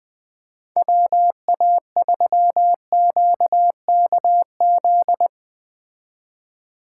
Morse code, heard as a keyed tone, about 21 wpm, 700 Hz. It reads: WA3QKZ